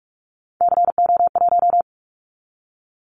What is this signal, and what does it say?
Morse code, heard as a keyed tone, about 44 wpm, 700 Hz. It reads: CO1
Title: Callsign Prefixes for Top 100 DXCC - 45wpm